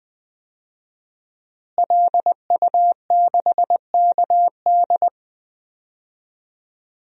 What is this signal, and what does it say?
Morse code, heard as a keyed tone, about 20 wpm, 700 Hz. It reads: LU6KD
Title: DXCC-100 Call Signs - 5 Characters - Repeated 2x - 20wpm